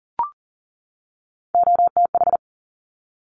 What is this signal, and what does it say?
Morse code, heard as a keyed tone, about 40 wpm, 700 Hz. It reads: OTH
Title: Top Three Letter Combinations in Words 40wpm